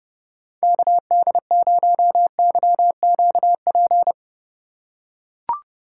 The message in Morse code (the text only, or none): KD0YQP